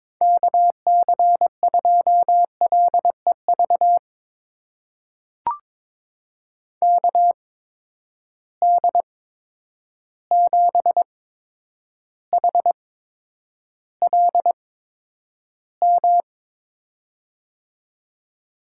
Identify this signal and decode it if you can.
Morse code, heard as a keyed tone, about 22 wpm, spaced as for 9 wpm, 700 Hz. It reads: KC2LEV KD7HLM
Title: USA Call Signs - 6 Characters - 8x Character Spacing - 22wpm